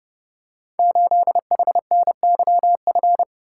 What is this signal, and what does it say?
Morse code, heard as a keyed tone, about 30 wpm, 700 Hz. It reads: 8HNYF
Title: Code Groups - 5 Letter:Numbers - 30wpm